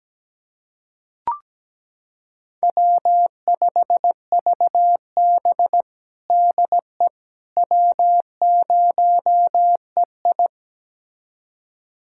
Morse code, heard as a keyed tone, about 17 wpm, 700 Hz. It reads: W5VB DE W0EI
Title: USA Callsign DE Callsign - 4 Characters - 17wpm